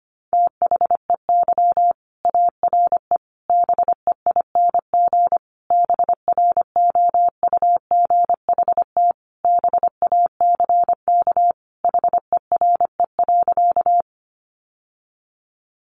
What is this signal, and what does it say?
Morse code, heard as a keyed tone, about 25 wpm, 700 Hz. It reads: THEY ARE BEING BROUGHT BACK HERE.
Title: Sentences from Top 600 Words 25wpm